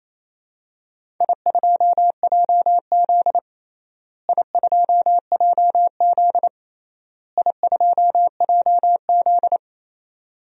Morse code, heard as a keyed tone, about 28 wpm, 700 Hz. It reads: I2JZ I2JZ I2JZ
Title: DXCC-100 Call Signs - 4 Characters - Repeated 3x - 28wpm